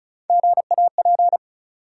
Morse code, heard as a keyed tone, about 36 wpm, 700 Hz. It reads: GAP
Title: Three Letter Words - 35wpm